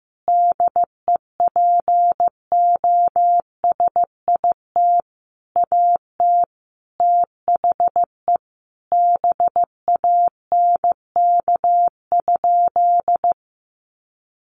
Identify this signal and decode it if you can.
Morse code, heard as a keyed tone, about 15 wpm, 700 Hz. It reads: DEPOSIT AT THE BANK?
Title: Sentences from Top 1300 Words 15wpm